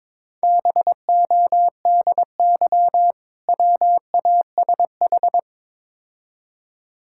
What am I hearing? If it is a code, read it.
Morse code, heard as a keyed tone, about 22 wpm, 690 Hz. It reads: BODY WASH
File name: Easy Words - Speed-Racing with Hints - 22wpm